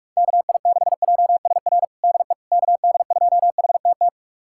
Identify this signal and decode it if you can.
Morse code, heard as a keyed tone, about 46 wpm, 700 Hz. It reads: KI6JSR DE KD2HTT
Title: USA Callsign DE Callsign - 6 Characters - 45wpm